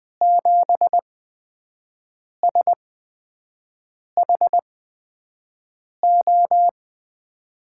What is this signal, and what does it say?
Morse code, heard as a keyed tone, about 20 wpm, spaced as for 5 wpm, 700 Hz. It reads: 7SHO
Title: USA Call Signs - 6 Characters - 8x Character Spacing - 20wpm